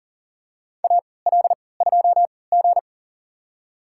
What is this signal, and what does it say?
Morse code, heard as a keyed tone, about 40 wpm, spaced as for 23 wpm, 700 Hz. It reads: AL2G
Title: USA Call Signs - 4 Characters - 3x Character Spacing - 40wpm